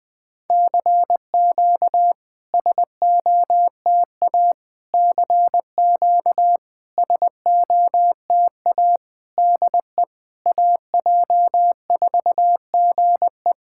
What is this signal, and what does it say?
Morse code, heard as a keyed tone, about 21 wpm, 700 Hz. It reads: CQ SOTA CQ SOTA DE AJ4GE
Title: SOTA QSO Practice 20wpm